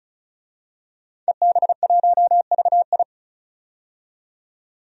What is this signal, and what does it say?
Morse code, heard as a keyed tone, about 36 wpm, 700 Hz. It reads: EB1VI
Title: DXCC-100 Call Signs - 5 Characters - 35wpm